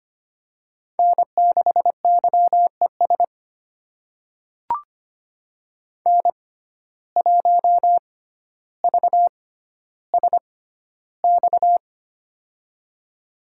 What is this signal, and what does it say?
Morse code, heard as a keyed tone, about 25 wpm, spaced as for 13 wpm, 700 Hz. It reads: N6YES N1VSX
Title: USA Call Signs - 5 Characters - 6x Character Spacing - 25wpm